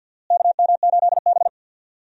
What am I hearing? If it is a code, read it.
Morse code, heard as a keyed tone, about 46 wpm, 700 Hz. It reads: KM8B
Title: USA Call Signs - 4 Characters - Repeated 2x - 50wpm